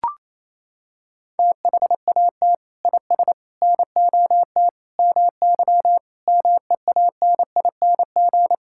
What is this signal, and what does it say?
Morse code, heard as a keyed tone, about 28 wpm, 700 Hz. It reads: THAT IS NOT MY MEANING
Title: Sentences from Top 1000 Words - Review Part 6 - 28wpm